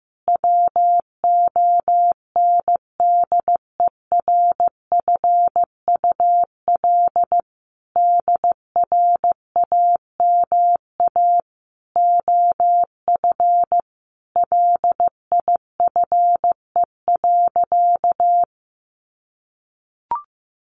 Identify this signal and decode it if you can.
Morse code, heard as a keyed tone, about 15 wpm, 700 Hz. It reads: WONDERFUL DRAMA OF LIFE.
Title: Sentences from Top 1500 Words 15wpm